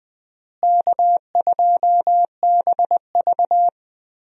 Morse code, heard as a keyed tone, about 20 wpm, 700 Hz. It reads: K2BV
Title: USA Call Signs - 4 Characters - Repeated 2x - 20wpm